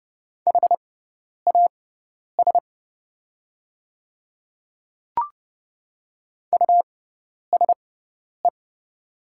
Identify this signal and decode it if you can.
Morse code, heard as a keyed tone, about 29 wpm, spaced as for 10 wpm, 700 Hz. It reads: HAS USE
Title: 6x Character Spacing - Top 100 Words - 30wpm